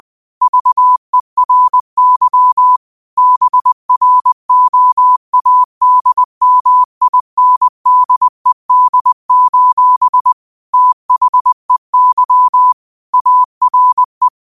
Morse code, heard as a keyed tone, about 20 wpm, 1000 Hz. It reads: VERY BROADMINDED: THEY ARE
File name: qotdfiles / 20wpm